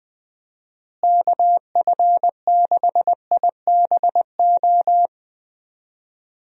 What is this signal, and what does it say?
Morse code, heard as a keyed tone, about 20 wpm, 700 Hz. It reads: KF6IBO
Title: USA Call Signs - 6 Characters - Repeated 2x - 20wpm